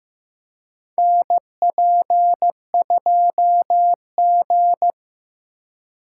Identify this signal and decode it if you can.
Morse code, heard as a keyed tone, about 15 wpm, 700 Hz. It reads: NP2G
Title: USA Call Signs - 4 Characters - 4x Character Spacing - 15wpm